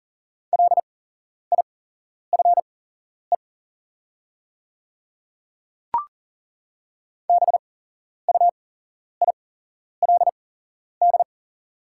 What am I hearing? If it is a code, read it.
Morse code, heard as a keyed tone, about 40 wpm, spaced as for 11 wpm, 700 Hz. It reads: LIFE BUILD
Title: 8x Character Spacing - Top 500 Words - 40wpm